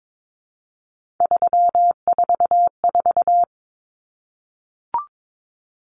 Morse code, heard as a keyed tone, about 22 wpm, 700 Hz. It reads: 344
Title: Numbers - 3-Digit 22wpm